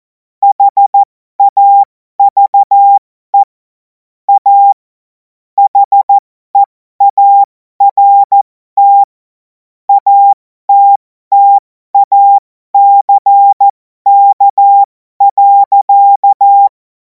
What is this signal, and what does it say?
Morse code, heard as a keyed tone, about 14 wpm, spaced as for 12 wpm, 800 Hz. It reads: HAVE A HEART ATTACK.
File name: Morsekurs / Witze